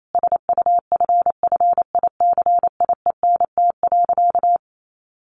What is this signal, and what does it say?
Morse code, heard as a keyed tone, about 28 wpm, 700 Hz. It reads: SUFFICIENT.
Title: Sentences from Top 1400 Words 28wpm